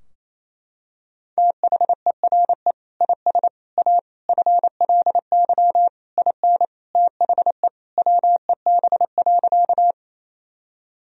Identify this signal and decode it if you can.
Morse code, heard as a keyed tone, about 28 wpm, 700 Hz. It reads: THERE IS A FLY IN THE WEB.
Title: Sentences from Top 1000 Words - Review Part 4 - 28wpm